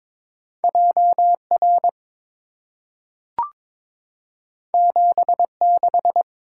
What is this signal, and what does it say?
Morse code, heard as a keyed tone, about 22 wpm, 690 Hz. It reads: JR 76
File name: practice / 22wpm / Two Letter-Number Combinations - 22wpm